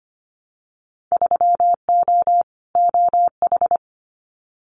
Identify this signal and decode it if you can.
Morse code, heard as a keyed tone, about 25 wpm, 700 Hz. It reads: 3O OH
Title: Winter Field Day Exchanges 25wpm